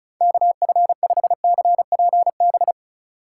Morse code, heard as a keyed tone, about 34 wpm, 690 Hz. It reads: KF5CPB
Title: USA Call Signs - 6 Characters - 4x Character Spacing - 35wpm